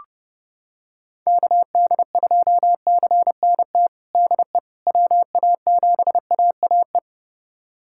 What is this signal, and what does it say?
Morse code, heard as a keyed tone, about 31 wpm, 700 Hz. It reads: KD2CNT DE WA7AAE
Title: USA Callsign DE Callsign - 6 Characters - 30wpm